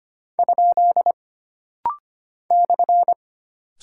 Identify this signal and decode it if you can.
Morse code, heard as a keyed tone, about 25 wpm, 700 Hz. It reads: ? /
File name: Single Punctuation-Prosign - Rapid-Fire - 25wpm